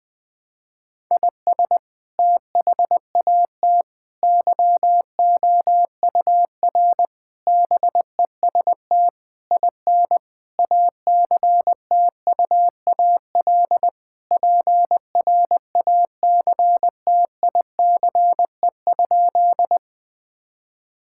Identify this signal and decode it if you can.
Morse code, heard as a keyed tone, about 21 wpm, 700 Hz. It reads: IS THAT YOUR BEST IN ACTUAL PRACTICE?
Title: Sentences from Top 1000 Words - Review Part 5 - 20wpm